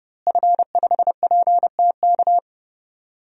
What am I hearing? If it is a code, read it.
Morse code, heard as a keyed tone, about 32 wpm, 700 Hz. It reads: F5PTK
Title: DXCC-100 Call Signs - 5 Characters - Repeated 2x - 30wpm